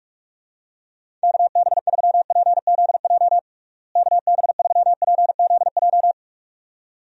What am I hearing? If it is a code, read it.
Morse code, heard as a keyed tone, about 41 wpm, 700 Hz. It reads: KB3PZJ KB3PZJ
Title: USA Call Signs - 6 Characters - Repeated 2x - 45wpm